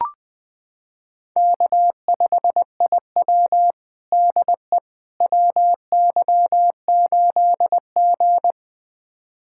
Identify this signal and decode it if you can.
Morse code, heard as a keyed tone, about 20 wpm, 700 Hz. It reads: K5IW DE WY8G
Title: USA Callsign DE Callsign - 4 Characters - 20wpm